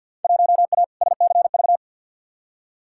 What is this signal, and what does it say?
Morse code, heard as a keyed tone, about 51 wpm, 700 Hz. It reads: 1A SCV
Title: Field Day Exchanges - 50wpm